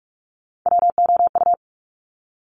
Morse code, heard as a keyed tone, about 46 wpm, 690 Hz. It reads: WOU
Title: Top Three Letter Combinations in Words 45wpm